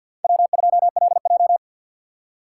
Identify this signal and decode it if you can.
Morse code, heard as a keyed tone, about 50 wpm, 700 Hz. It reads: W2LJ
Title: USA Call Signs - 4 Characters - 2x Character Spacing - 50wpm